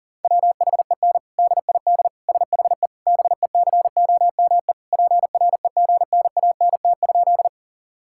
Morse code, heard as a keyed tone, about 40 wpm, 700 Hz. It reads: WHEN DID SHE BECOME PREGNANT?
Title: Sentences from Top 1000 Words - Review Part 6 - 40wpm